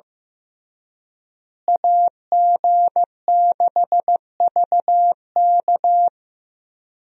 Morse code, heard as a keyed tone, about 15 wpm, 700 Hz. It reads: AG6VK
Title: USA Call Signs - 5 Characters - Repeated 3x - 15wpm